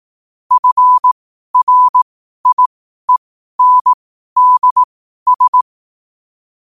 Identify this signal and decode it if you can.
Morse code, heard as a keyed tone, about 18 wpm, spaced as for 12 wpm, 1000 Hz. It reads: FRIENDS
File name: qotdfiles / 12wpm